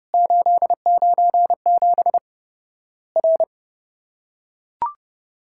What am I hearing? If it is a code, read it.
Morse code, heard as a keyed tone, about 30 wpm, 700 Hz. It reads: 897 R
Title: Worked All Europe DX Contest - QTC Exchange - 30wpm